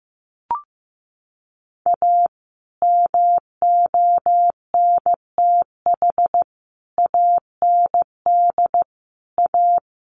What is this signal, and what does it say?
Morse code, heard as a keyed tone, about 15 wpm, 700 Hz. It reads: A MONTH AND A
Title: Sentences from Top 800 Words 15wpm